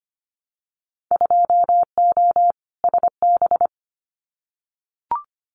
Morse code, heard as a keyed tone, about 25 wpm, 700 Hz. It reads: 2O SB
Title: Winter Field Day Exchanges 25wpm